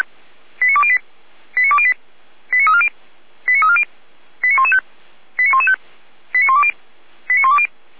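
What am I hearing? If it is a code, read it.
Five-tone selective call (ZVEI), heard as quick runs of five tones, tones of about 70 ms. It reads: 89208 89208 89340 89340 89115 89115 89120 89120
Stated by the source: ZVEI five-tone selective call 89094 received by SDR.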